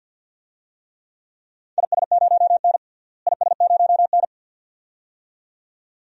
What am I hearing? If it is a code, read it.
Morse code, heard as a keyed tone, about 47 wpm, 700 Hz. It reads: IS0N IS0N
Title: DXCC-100 Call Signs - 4 Characters - Repeated 2x - 50wpm